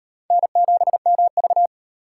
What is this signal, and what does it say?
Morse code, heard as a keyed tone, about 38 wpm, 700 Hz. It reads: N7MV